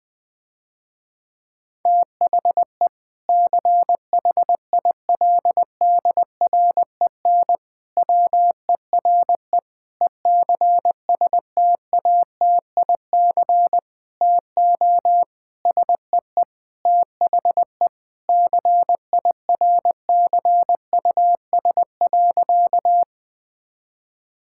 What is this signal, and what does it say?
Morse code, heard as a keyed tone, about 20 wpm, 700 Hz. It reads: THE CHILDREN WERE ECSTATIC TO SEE THE CIRCUS.
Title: Difficult Sentences with Hints - 20wpm